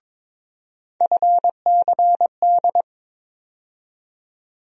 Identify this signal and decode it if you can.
Morse code, heard as a keyed tone, about 21 wpm, 700 Hz. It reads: FCD
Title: USA Call Sign Postfixes - 3 Letters 22wpm